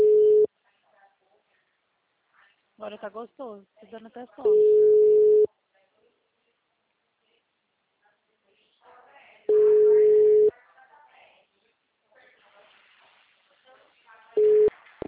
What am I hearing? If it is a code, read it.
Ringback tone (Europe, 425 Hz), with speech.